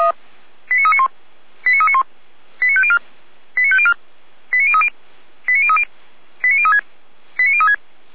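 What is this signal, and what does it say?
Five-tone selective call (ZVEI), heard as quick runs of five tones, tones of about 70 ms. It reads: 89391 89391 89504 89504 89030 89030 89036 89036
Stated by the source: Five-tone ZVEI selective call received on SDR.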